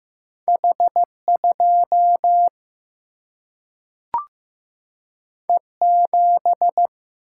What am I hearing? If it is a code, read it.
Morse code, heard as a keyed tone, about 15 wpm, 700 Hz. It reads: H2 E7
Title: Single-Letter Single-Number 15wpm